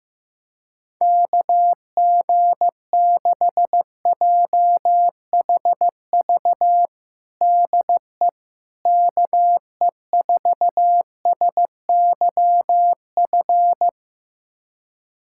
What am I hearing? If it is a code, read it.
Morse code, heard as a keyed tone, about 15 wpm, 700 Hz. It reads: KG6JHV DE KE4SYF
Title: USA Callsign DE Callsign - 6 Characters - 15wpm